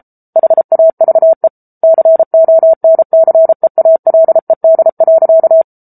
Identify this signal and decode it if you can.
Morse code, heard as a keyed tone, about 34 wpm, 650 Hz. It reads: HAVE CONCEALED.
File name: Book Two / 07-Chapter 7 - 33 WPM